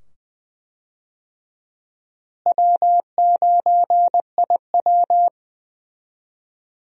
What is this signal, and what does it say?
Morse code, heard as a keyed tone, about 20 wpm, 700 Hz. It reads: W9IW
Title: USA Call Signs - 4 Characters - Repeated 2x - 20wpm